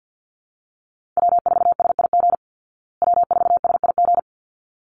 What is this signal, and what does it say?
Morse code, heard as a keyed tone, about 49 wpm, 700 Hz. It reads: W4SIG W4SIG
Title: Call Signs Repeated 3x - 50wpm